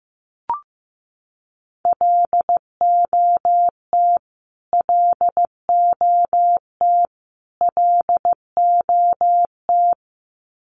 Morse code, heard as a keyed tone, about 15 wpm, 700 Hz. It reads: LOT LOT LOT
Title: Top 500 Words Repeated 15wpm